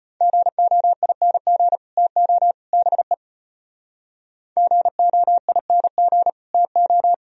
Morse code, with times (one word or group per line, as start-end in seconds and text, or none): 0.2-1.8 GOING
2.0-2.5 TO
2.7-3.1 BE
4.6-6.3 GOING
6.5-7.1 TO